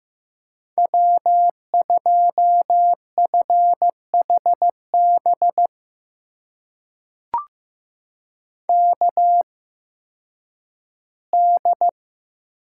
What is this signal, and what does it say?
Morse code, heard as a keyed tone, about 16 wpm, spaced as for 10 wpm, 700 Hz. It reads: W2FHB KD
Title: USA Call Signs - 8x Character Spacing - 15wpm